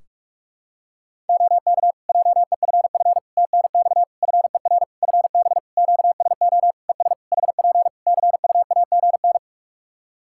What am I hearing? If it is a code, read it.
Morse code, heard as a keyed tone, about 46 wpm, 700 Hz. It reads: OK JEFF TNX FER FB QSO ES HP CUAGN
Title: Practice QSO - Exchange 4 - Ending - 45wpm